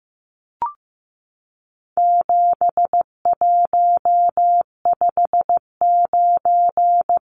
Morse code, heard as a keyed tone, about 15 wpm, 700 Hz. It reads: 7159
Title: Numbers - 4-Digit 15wpm